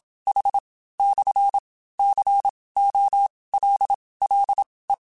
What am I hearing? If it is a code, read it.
Morse code, heard as a keyed tone, about 27 wpm, spaced as for 18 wpm, 800 Hz. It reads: H/COLLE
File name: QSOs / QSOs-mp3s / WD0FBT 1@25